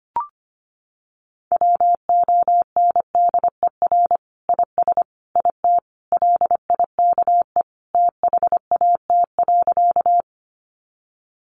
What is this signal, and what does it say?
Morse code, heard as a keyed tone, about 25 wpm, 700 Hz. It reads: WONDER IS IT LIKE THAT.
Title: Sentences from Top 500 Words 25wpm